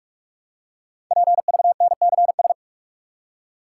Morse code, heard as a keyed tone, about 44 wpm, 700 Hz. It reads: PVNCS
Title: Code Groups - 5 Letter:Numbers - 45wpm